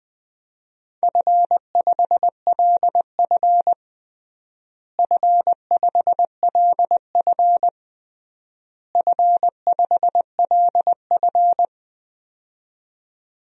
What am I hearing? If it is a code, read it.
Morse code, heard as a keyed tone, about 21 wpm, 700 Hz. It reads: F5LF F5LF F5LF
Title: DXCC-100 Call Signs - 4 Characters - Repeated 3x - 20wpm